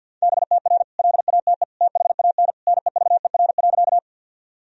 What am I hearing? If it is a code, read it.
Morse code, heard as a keyed tone, about 48 wpm, 700 Hz. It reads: BTR LATE THAN NEVER.
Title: Sentences from Top 400 Words and Abbreviations 50wpm